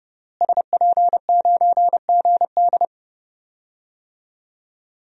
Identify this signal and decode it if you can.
Morse code, heard as a keyed tone, about 30 wpm, 700 Hz. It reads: SP9GD
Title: DXCC-100 Call Signs - 5 Characters - Repeated 3x - 30wpm